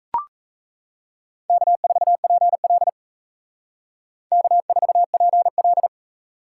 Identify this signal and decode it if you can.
Morse code, heard as a keyed tone, about 35 wpm, 700 Hz. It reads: K4PL K4PL
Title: Speed Racing - USA Call Signs 28wpm